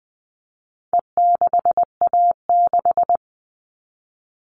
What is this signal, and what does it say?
Morse code, heard as a keyed tone, about 20 wpm, 700 Hz. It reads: E6A6
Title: All Cut Numbers - 4-Digit 20wpm